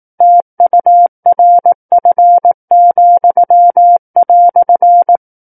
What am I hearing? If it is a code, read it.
Morse code, heard as a keyed tone, about 18 wpm, 700 Hz. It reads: TURF,"
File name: Book One / 09-Chapter 9 - 18 WPM